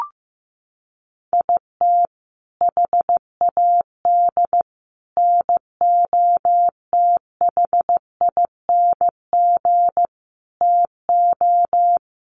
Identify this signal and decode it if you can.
Morse code, heard as a keyed tone, about 15 wpm, 700 Hz. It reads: IT HAD NOTHING TO
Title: Sets of 5 Words Encore2 15wpm